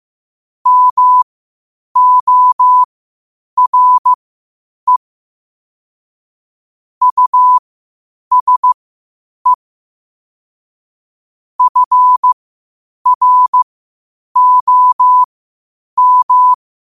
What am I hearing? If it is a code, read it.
Morse code, heard as a keyed tone, about 15 wpm, spaced as for 8 wpm, 1000 Hz. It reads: MORE USE FROM
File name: qotdfiles / 8wpm